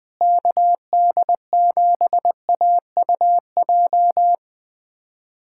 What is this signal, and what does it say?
Morse code, heard as a keyed tone, about 21 wpm, 700 Hz. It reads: KD7AUJ
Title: USA Callsign DE Callsign - 6 Characters - 20wpm